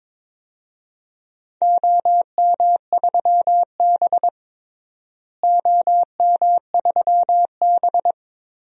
Morse code, heard as a keyed tone, about 22 wpm, 700 Hz. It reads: OM3B OM3B
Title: DXCC-100 Call Signs - 4 Characters - Repeated 3x - 22wpm